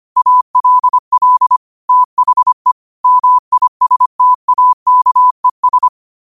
Morse code, heard as a keyed tone, about 25 wpm, 1000 Hz. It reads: ALL THE MISTAKES
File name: qotdfiles / 25wpm